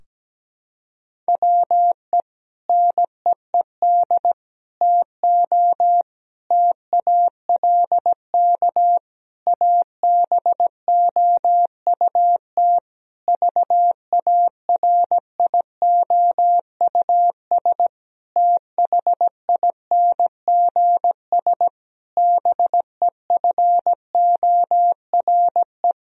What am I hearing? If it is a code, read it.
Morse code, heard as a keyed tone, about 17 wpm, 700 Hz. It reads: WE NEED TO TALK ABOUT VARIOUS THINGS BEFORE
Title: Sentences from Top 1000 Words - Review Part 4 - 17wpm